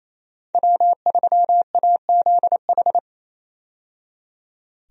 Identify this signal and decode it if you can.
Morse code, heard as a keyed tone, about 27 wpm, 700 Hz. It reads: W3AZH
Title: USA Call Signs - 5 Characters - 28wpm